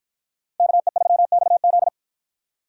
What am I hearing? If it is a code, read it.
Morse code, heard as a keyed tone, about 53 wpm, 700 Hz. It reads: KE3XZ